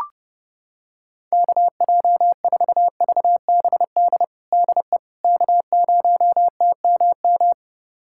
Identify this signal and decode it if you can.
Morse code, heard as a keyed tone, about 30 wpm, 700 Hz. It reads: KJ4VBD DE K0TMM
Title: USA Callsign DE Callsign 30wpm